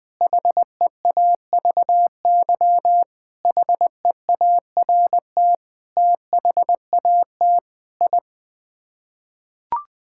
Morse code, heard as a keyed tone, about 21 wpm, 700 Hz. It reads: HEAVY HEART THAT I
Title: Sets of 8 Words - 20wpm